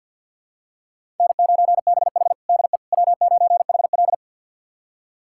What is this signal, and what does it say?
Morse code, heard as a keyed tone, about 45 wpm, 700 Hz. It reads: N9BH DE W9HL
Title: USA Callsign DE Callsign - 4 Characters - 50wpm